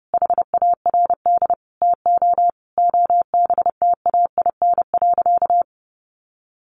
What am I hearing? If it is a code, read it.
Morse code, heard as a keyed tone, about 30 wpm, 700 Hz. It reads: HARD TO OBTAIN.
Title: Sentences from Top 1300 Words 30wpm